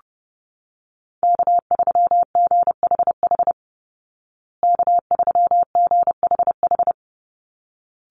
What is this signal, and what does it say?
Morse code, heard as a keyed tone, about 30 wpm, 700 Hz. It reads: K3GHH K3GHH
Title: Call Signs Repeated 3x - 30wpm